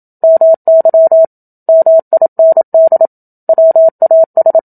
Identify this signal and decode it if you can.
Morse code, heard as a keyed tone, about 27 wpm, 650 Hz. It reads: MY MIND WAS
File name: Book Two / 01-Chapter 1 - 27 WPM